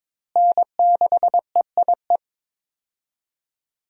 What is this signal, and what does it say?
Morse code, heard as a keyed tone, about 22 wpm, 700 Hz. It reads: N6EIE
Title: USA Call Signs - 5 Characters - Repeated 3x - 22wpm